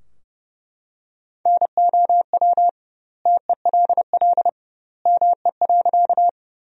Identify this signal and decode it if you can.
Morse code, heard as a keyed tone, about 30 wpm, 700 Hz. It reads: NOW TELL ME.
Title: 2x Word-Spacing - Sentences from Top 200 Words - 30wpm